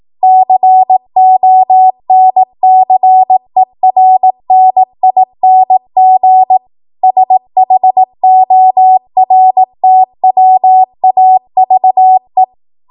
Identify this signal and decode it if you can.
Morse code, heard as a keyed tone, about 18 wpm, 750 Hz. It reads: CONCERNING SHORTWAVE
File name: Archive / 18 WPM / 18WPM